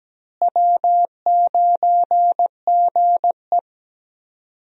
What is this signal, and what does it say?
Morse code, heard as a keyed tone, about 17 wpm, 700 Hz. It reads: W9GE
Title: USA Call Signs - 4 Characters - Repeated 2x - 17wpm